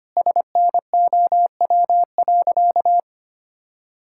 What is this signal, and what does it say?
Morse code, heard as a keyed tone, about 25 wpm, 700 Hz. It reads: SNOW.
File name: practice / 25wpm / Sentences with 4-Word Phrases - 25wpm